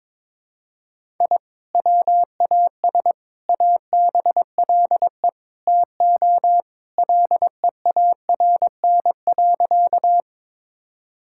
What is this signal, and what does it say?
Morse code, heard as a keyed tone, about 22 wpm, 700 Hz. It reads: I WAS ABLE TO LEARN.
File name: practice / 22wpm / Sentences with 4-Word Phrases - 22wpm